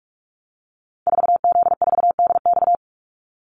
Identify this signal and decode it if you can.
Morse code, heard as a keyed tone, about 44 wpm, 700 Hz. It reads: 4Z4DX
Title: Call Signs Repeated 2x - 45wpm